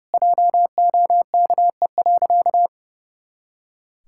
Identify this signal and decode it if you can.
Morse code, heard as a keyed tone, about 30 wpm, 700 Hz. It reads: JOKE.
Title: Pangram Challenge v2 - 30wpm